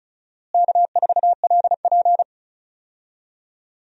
Morse code, heard as a keyed tone, about 34 wpm, 700 Hz. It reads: K4LP
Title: USA Call Signs - 4 Characters - Repeated 2x - 35wpm